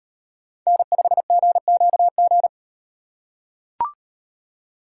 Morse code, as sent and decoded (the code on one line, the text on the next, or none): -. ..... --. --.- --.
N5GQG